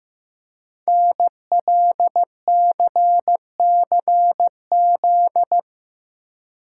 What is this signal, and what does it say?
Morse code, heard as a keyed tone, about 15 wpm, 700 Hz. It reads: NLCCZ
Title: Code Groups - 5 Letters - 15wpm